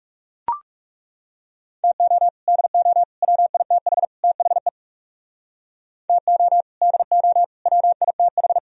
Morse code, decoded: TO DO WITH THE TO DO WITH

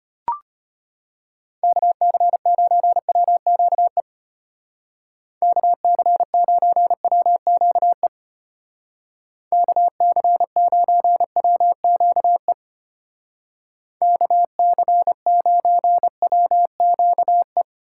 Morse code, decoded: KC9WQE KC9WQE KC9WQE KC9WQE

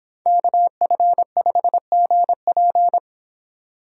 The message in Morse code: KF5GP

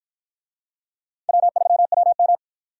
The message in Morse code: W3WM

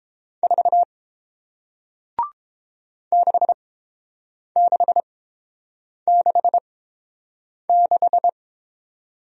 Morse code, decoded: 4 6666

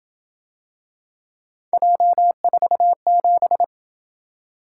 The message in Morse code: J47

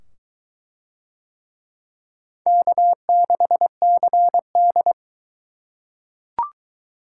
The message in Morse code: K6CD

